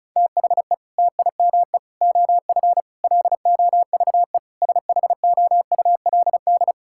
Morse code, heard as a keyed tone, about 34 wpm, 700 Hz. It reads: THE TIME OF LOVE SHOULD
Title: Sentences from Top 1600 Words 35wpm